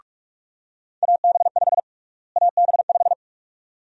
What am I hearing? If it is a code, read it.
Morse code, heard as a keyed tone, about 46 wpm, 700 Hz. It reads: AB5 AB5
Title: USA Call Sign Prefixes - Repeated 2x - 45wpm